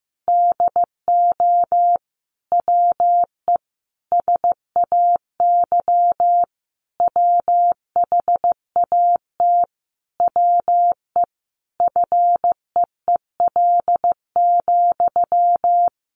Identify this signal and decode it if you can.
Morse code, heard as a keyed tone, about 15 wpm, 700 Hz. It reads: DO WE SAY WHAT WE FEEL,
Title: Sentences from Top 1200 Words 15wpm